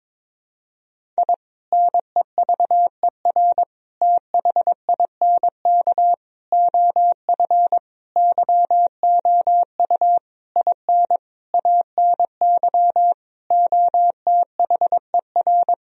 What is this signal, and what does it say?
Morse code, heard as a keyed tone, about 22 wpm, 700 Hz. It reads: I NEVER THINK OF YOU IN ANY OTHER
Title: Sentences from Top 1000 Words - Review Part 1 - 22wpm